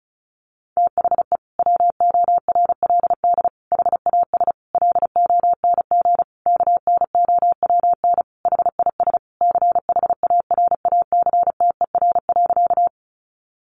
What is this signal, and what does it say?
Morse code, heard as a keyed tone, about 34 wpm, 700 Hz. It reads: THE WORLD HAS LONG KNOWN HIS CHARACTER.
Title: Sentences from Top 1200 Words 35wpm